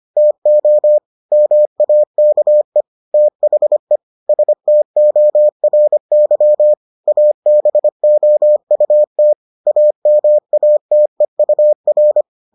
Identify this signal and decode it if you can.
Morse code, heard as a keyed tone, about 25 wpm, 600 Hz. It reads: TO MAKE THE STORY ABOUT AMATEUR